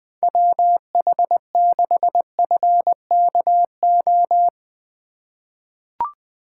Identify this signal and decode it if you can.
Morse code, heard as a keyed tone, about 20 wpm, 700 Hz. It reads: WH6FKO
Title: USA Call Signs - 6 Characters - 3x Character Spacing - 20wpm